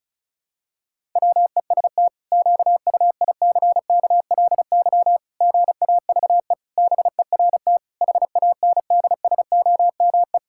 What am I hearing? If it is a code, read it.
Morse code, heard as a keyed tone, about 34 wpm, 700 Hz. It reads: WEST QUICKLY GAVE BERT HANDSOME
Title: Pangram Challenge v2 - 35wpm